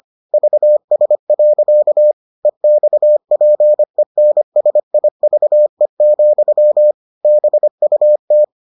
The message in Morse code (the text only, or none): VS. EXPENSIVE, BUT